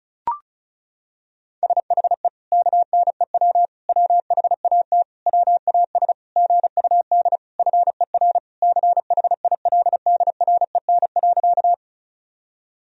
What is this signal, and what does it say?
Morse code, heard as a keyed tone, about 36 wpm, 700 Hz. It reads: SHE KNEW WHAT WAS GUD FER CHILDREN.
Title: Sentences from Top 400 Words and Abbreviations 35wpm